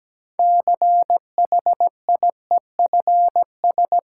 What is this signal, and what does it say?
Morse code, heard as a keyed tone, about 17 wpm, 700 Hz. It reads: CHIEFS
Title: Sets of 5 Words Encore - New Word Drill - 17wpm